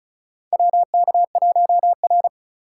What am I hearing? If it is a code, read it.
Morse code, heard as a keyed tone, about 36 wpm, 700 Hz. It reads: WK1R
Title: USA Call Signs - 3x Character Spacing - 35wpm